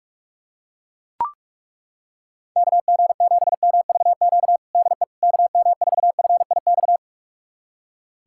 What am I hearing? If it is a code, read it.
Morse code, heard as a keyed tone, about 42 wpm, 700 Hz. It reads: KG7MVQ DE KM4FIX
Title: USA Callsign DE Callsign 45wpm